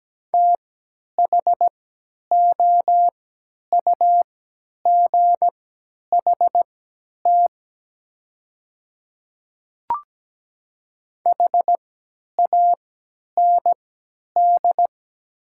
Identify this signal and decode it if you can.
Morse code, heard as a keyed tone, about 17 wpm, spaced as for 10 wpm, 700 Hz. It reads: THOUGHT HAND
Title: 3x Character Spacing - Top 500 Words - 17wpm